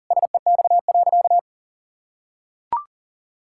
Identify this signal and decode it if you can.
Morse code, heard as a keyed tone, about 37 wpm, 700 Hz. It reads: SEX.
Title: Sentences from Top 1600 Words 40wpm